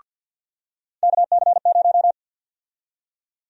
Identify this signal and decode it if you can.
Morse code, heard as a keyed tone, about 51 wpm, 700 Hz. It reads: KC0
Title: USA Call Sign Prefixes - 50wpm